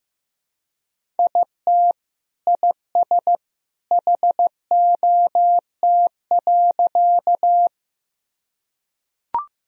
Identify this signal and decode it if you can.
Morse code, heard as a keyed tone, about 15 wpm, 700 Hz. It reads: IT IS HOT.
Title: Sentences from Top 1000 Words - Review Part 3 - 15wpm